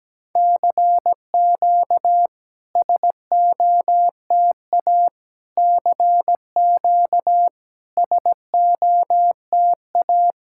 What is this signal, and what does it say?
Morse code, heard as a keyed tone, about 17 wpm, 700 Hz. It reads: CQ SOTA CQ SOTA
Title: SOTA QSO Practice 17wpm